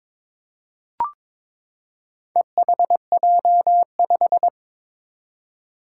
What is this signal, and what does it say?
Morse code, heard as a keyed tone, about 22 wpm, 700 Hz. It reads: EHJ5
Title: Four Challenging Letter-Number Combinations 22wpm